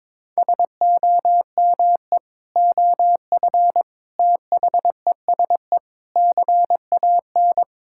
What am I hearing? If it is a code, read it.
Morse code, heard as a keyed tone, about 22 wpm, 700 Hz. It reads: SOME OF THESE CAN
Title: Sentences from Top 1000 Words - Review Part 6 - 22wpm